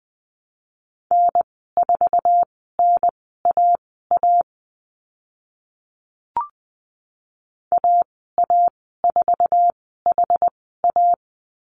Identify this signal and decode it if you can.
Morse code, heard as a keyed tone, about 20 wpm, spaced as for 14 wpm, 700 Hz. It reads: N4NAA AA4HA
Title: 069 Lesson 12g - Letter d - Call Signs 20wpm